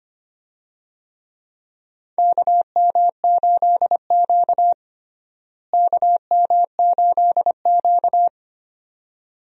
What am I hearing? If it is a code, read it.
Morse code, heard as a keyed tone, about 25 wpm, 700 Hz. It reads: KM8Q KM8Q
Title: USA Call Signs - 4 Characters - Repeated 2x - 25wpm